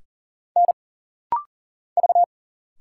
Morse code, heard as a keyed tone, about 40 wpm, 700 Hz. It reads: N V
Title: Single Letters - Rapid-Fire 40wpm